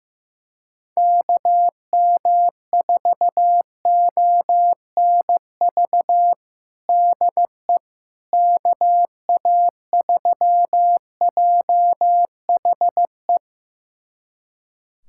Morse code, heard as a keyed tone, about 15 wpm, 700 Hz. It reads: KM4ONV DE KA3JHE
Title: USA Callsign DE Callsign - 6 Characters - 15wpm